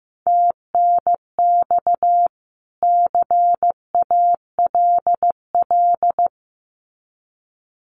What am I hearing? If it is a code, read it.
Morse code, heard as a keyed tone, about 15 wpm, 700 Hz. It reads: TNX CALL
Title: Common QSO Phrases 15wpm